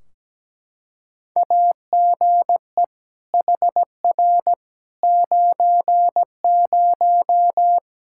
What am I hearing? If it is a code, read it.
Morse code, heard as a keyed tone, about 17 wpm, 700 Hz. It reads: AGE HR 90
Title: K1USN SST QSO Practice 17wpm